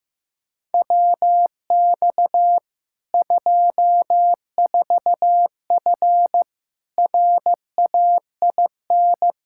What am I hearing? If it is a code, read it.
Morse code, heard as a keyed tone, about 15 wpm, 700 Hz. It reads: WX 24F RAIN
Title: Practice QSO - Exchange 2 - Info - 15wpm